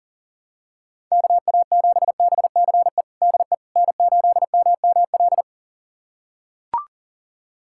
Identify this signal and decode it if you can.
Morse code, heard as a keyed tone, about 37 wpm, 700 Hz. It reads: KA7BCE DE N8MML
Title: USA Callsign DE Callsign 40wpm